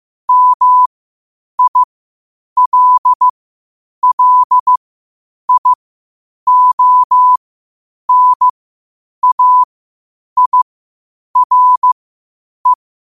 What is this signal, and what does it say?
Morse code, heard as a keyed tone, about 15 wpm, spaced as for 8 wpm, 1000 Hz. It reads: MILLIONAIRE
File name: qotdfiles / 8wpm